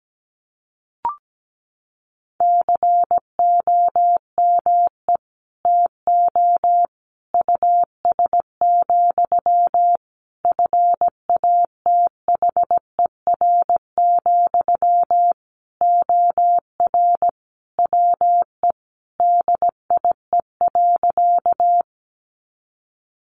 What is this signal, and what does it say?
Morse code, heard as a keyed tone, about 17 wpm, 700 Hz. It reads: COME TO US, FATHER, OR WE DIE.
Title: Sentences from Top 1300 Words 17wpm